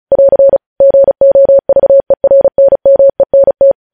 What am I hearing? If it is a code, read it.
Morse code, heard as a keyed tone, about 36 wpm, 550 Hz. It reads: <AR> GOVERNMENT
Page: news headlines in morse code at 35 wpm